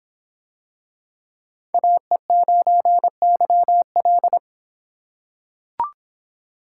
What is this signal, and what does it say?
Morse code, heard as a keyed tone, about 24 wpm, 700 Hz. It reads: AE9YL